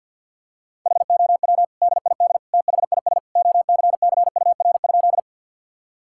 Morse code, heard as a keyed tone, about 46 wpm, 700 Hz. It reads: HOW DID THIS OCCUR?
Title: Sentences from Top 1000 Words - Review Part 6 - 50wpm